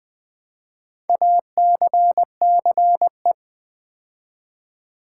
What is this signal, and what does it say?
Morse code, heard as a keyed tone, about 20 wpm, 700 Hz. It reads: ACCE
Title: Build A Word - 2 Letters At A Time - 6-Letter Words - 20wpm